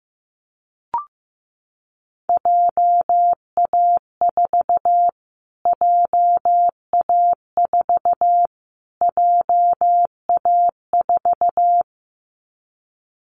Morse code, heard as a keyed tone, about 15 wpm, 700 Hz. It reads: JA4 JA4 JA4
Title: Callsign Prefixes for Top 25 DXCC - Repeated 3x - 15wpm